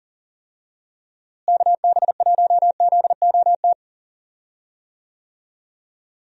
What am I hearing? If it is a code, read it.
Morse code, heard as a keyed tone, about 40 wpm, 700 Hz. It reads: KB1ZOT
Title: USA Call Signs - 6 Characters - Repeated 3x - 40wpm